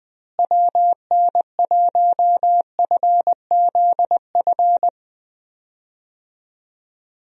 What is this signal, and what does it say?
Morse code, heard as a keyed tone, about 20 wpm, 700 Hz. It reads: WN1FZF
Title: USA Call Signs - 6 Characters - Repeated 2x - 20wpm